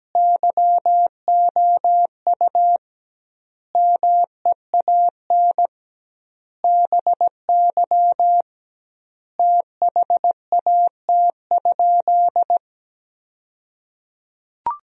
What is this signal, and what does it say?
Morse code, heard as a keyed tone, about 17 wpm, 700 Hz. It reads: YOU MEAN BY THAT?
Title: 2x Word-Spacing - Sentences from Top 200 Words - 17wpm